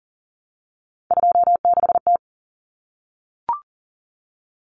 Morse code, heard as a keyed tone, about 40 wpm, 700 Hz. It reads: 26T
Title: Common Cut Numbers - 3-Digit 40wpm